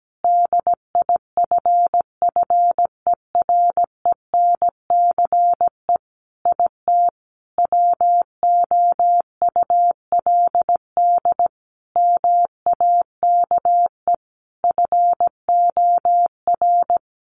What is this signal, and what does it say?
Morse code, heard as a keyed tone, about 17 wpm, 700 Hz. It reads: DIFFERENCE IT WOULD MAKE FOR
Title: Sentences from Top 1300 Words 17wpm